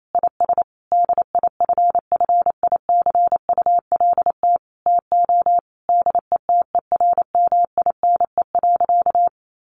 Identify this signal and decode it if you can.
Morse code, heard as a keyed tone, about 28 wpm, 700 Hz. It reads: IS DIFFICULT TO DETERMINE.
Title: Sentences from Top 1500 Words 28wpm